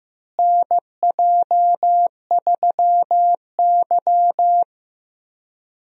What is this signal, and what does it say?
Morse code, heard as a keyed tone, about 15 wpm, 700 Hz. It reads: NJ3Y
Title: USA Call Signs - 4 Characters - Repeated 3x - 15wpm